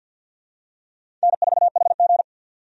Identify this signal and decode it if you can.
Morse code, heard as a keyed tone, about 52 wpm, 700 Hz. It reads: N4HG
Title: USA Call Signs - 4 Characters - 50wpm